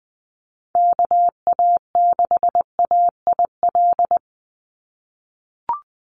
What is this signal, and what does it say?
Morse code, heard as a keyed tone, about 19 wpm, 700 Hz. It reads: KA6AIL
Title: Call Signs - 20wpm